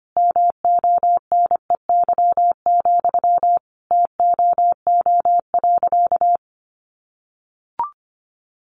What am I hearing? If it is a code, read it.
Morse code, heard as a keyed tone, about 25 wpm, 700 Hz. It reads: MONEY, TOO.
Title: Sentences from Top 900 Words 25wpm